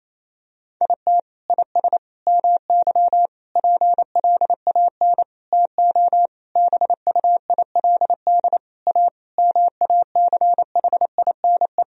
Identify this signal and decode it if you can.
Morse code, heard as a keyed tone, about 28 wpm, 700 Hz. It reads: IT IS MY PLAN TO BUILD A MACHINE
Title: Sentences from Top 1000 Words - Review Part 2 - 28wpm